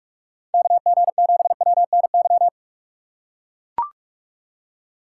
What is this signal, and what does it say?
Morse code, heard as a keyed tone, about 44 wpm, 700 Hz. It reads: KG7WNY
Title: USA Call Signs - 6 Characters - 6x Character Spacing - 45wpm